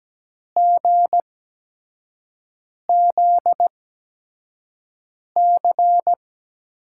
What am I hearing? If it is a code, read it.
Morse code, heard as a keyed tone, about 17 wpm, spaced as for 5 wpm, 700 Hz. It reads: GZC
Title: USA Call Signs - 6 Characters - 8x Character Spacing - 17wpm